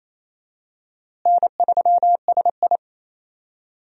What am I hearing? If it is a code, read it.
Morse code, heard as a keyed tone, about 28 wpm, 700 Hz. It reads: N3SI
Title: USA Call Signs - 4 Characters - Repeated 2x - 28wpm